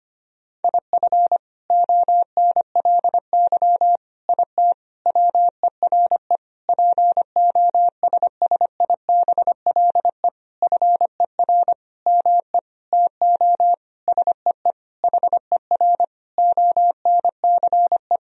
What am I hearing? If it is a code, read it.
Morse code, heard as a keyed tone, about 25 wpm, 700 Hz. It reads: IF ONLY IT WERE POSSIBLE FER ME TO SEE HER ONCE
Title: Sentences from Top 500 Words and Abbreviations 25wpm